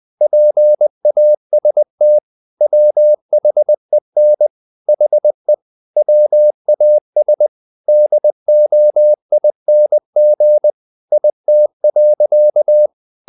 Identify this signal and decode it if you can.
Morse code, heard as a keyed tone, about 20 wpm, 600 Hz. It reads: PAST WHEN HE WAS DOING IT.
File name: the old man and the sea / the old man and the sea part 1 20wpm